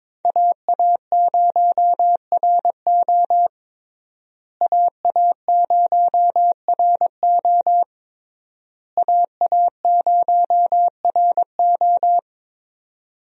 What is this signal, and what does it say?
Morse code, heard as a keyed tone, about 21 wpm, 700 Hz. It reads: AA0RO AA0RO AA0RO
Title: USA Call Signs - 5 Characters - Repeated 3x - 22wpm